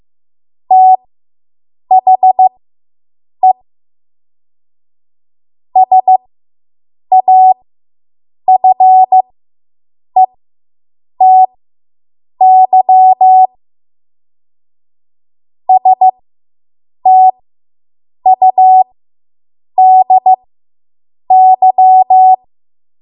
Morse code, heard as a keyed tone, about 15 wpm, spaced as for 7 wpm, 750 Hz. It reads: THE SAFETY STUDY